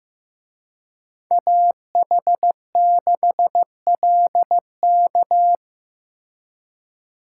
Morse code, heard as a keyed tone, about 18 wpm, 700 Hz. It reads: AH6LK
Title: Speed Racing - USA 2x2 Call Signs 15wpm